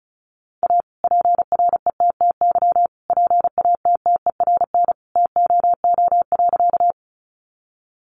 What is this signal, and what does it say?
Morse code, heard as a keyed tone, about 34 wpm, 700 Hz. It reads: A PRETTY PATTERN TOO.
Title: Sentences from Top 1000 Words 35wpm